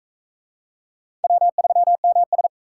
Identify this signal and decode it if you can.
Morse code, heard as a keyed tone, about 37 wpm, 700 Hz. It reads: W3MS